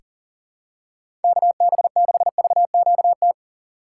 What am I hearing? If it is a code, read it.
Morse code, heard as a keyed tone, about 40 wpm, 700 Hz. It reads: KB6VQT
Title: USA Call Signs - 6 Characters - 4x Character Spacing - 40wpm